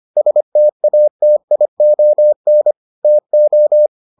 Morse code, heard as a keyed tone, about 25 wpm, 600 Hz. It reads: STATION TO